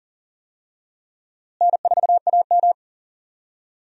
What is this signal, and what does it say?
Morse code, heard as a keyed tone, about 40 wpm, 700 Hz. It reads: N4AM
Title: USA Call Signs - 4 Characters - Repeated 3x - 40wpm